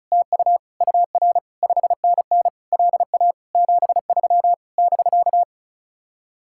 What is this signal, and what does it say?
Morse code, heard as a keyed tone, about 45 wpm, 700 Hz. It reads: TU UR 5NN LA 73 <BK>